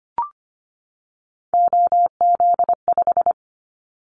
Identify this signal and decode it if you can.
Morse code, heard as a keyed tone, about 25 wpm, 700 Hz. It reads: OZ5
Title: Callsign Prefixes for Top 100 DXCC - 25wpm